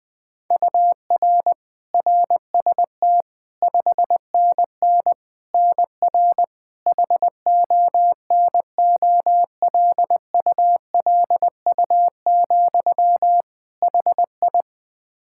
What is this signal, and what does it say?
Morse code, heard as a keyed tone, about 20 wpm, 700 Hz. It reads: UR RST 5NN NR HONOLULU, HI
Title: Practice QSO - Exchange 1 - Beginning - 20wpm